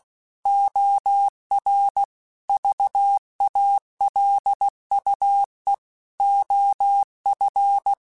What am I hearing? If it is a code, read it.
Morse code, heard as a keyed tone, about 16 wpm, 800 Hz. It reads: OR VALUE OF